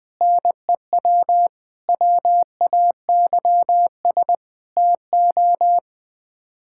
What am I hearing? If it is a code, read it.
Morse code, heard as a keyed tone, about 21 wpm, 700 Hz. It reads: NEW WAYS TO
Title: Sets of 6 Words - 20wpm